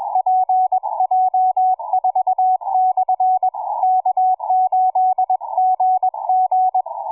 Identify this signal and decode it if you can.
Morse code, heard as a keyed tone, about 21 wpm, 740 Hz, with radio noise. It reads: PJ4/K8GG
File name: cw forum / pj4 k8gg 1827 160m 05h26